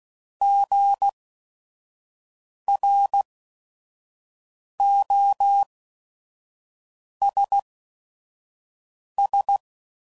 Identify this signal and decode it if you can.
Morse code, heard as a keyed tone, about 16 wpm, spaced as for 5 wpm, 800 Hz. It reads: GROSS
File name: Prefixes / By First Letter / Gs